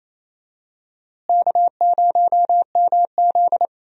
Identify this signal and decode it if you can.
Morse code, heard as a keyed tone, about 28 wpm, 700 Hz. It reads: K0MZ
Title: USA Call Signs - 4 Characters - Repeated 3x - 28wpm